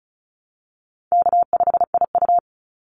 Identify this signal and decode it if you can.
Morse code, heard as a keyed tone, about 34 wpm, 700 Hz. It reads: K5IU
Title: Easy Call Signs Repeated 3x - 35wpm